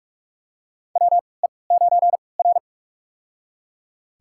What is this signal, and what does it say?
Morse code, heard as a keyed tone, about 46 wpm, spaced as for 26 wpm, 690 Hz. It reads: WE9R
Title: USA Call Signs - 4 Characters - 3x Character Spacing - 45wpm